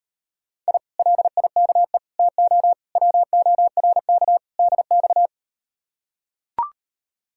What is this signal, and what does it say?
Morse code, heard as a keyed tone, about 39 wpm, 700 Hz. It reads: I LIKE TO WORK DX